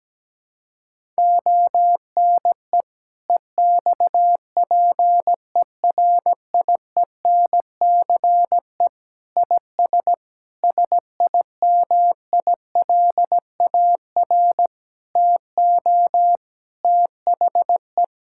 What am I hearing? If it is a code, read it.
Morse code, heard as a keyed tone, about 17 wpm, 700 Hz. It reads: ONE EXPERIENCE IS SIMILAR TO THE
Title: Sentences from Top 1000 Words - Review Part 5 - 17wpm